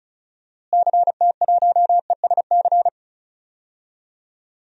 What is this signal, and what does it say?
Morse code, heard as a keyed tone, about 36 wpm, 700 Hz. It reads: CT1ESC
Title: DXCC-100 Call Signs - 6 Characters - Repeated 2x - 35wpm